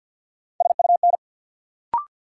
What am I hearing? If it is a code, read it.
Morse code, heard as a keyed tone, about 53 wpm, 700 Hz. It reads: SUN
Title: Sentences from Top 300 Words - New Word Drill - 50wpm